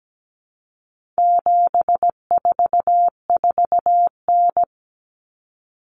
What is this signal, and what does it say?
Morse code, heard as a keyed tone, about 17 wpm, 700 Hz. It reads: 744N
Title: Common Cut Numbers - 4-Digit 17wpm